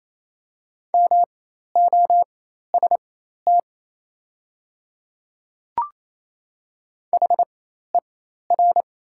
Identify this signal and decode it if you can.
Morse code, heard as a keyed tone, about 28 wpm, spaced as for 13 wpm, 700 Hz. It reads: MOST HER
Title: 4x Character Spacing - Top 100 Words - 28wpm